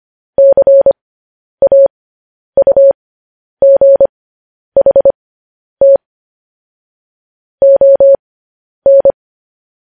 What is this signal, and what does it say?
Morse code, heard as a keyed tone, about 25 wpm, spaced as for 10 wpm, 550 Hz. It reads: CAUGHT ON
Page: news headlines in morse code at 10 wpm